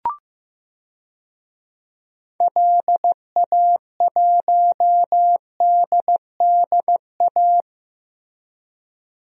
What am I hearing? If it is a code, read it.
Morse code, heard as a keyed tone, about 15 wpm, 700 Hz. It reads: LA1DDA
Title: DXCC-100 Call Signs - 6 Characters - 15wpm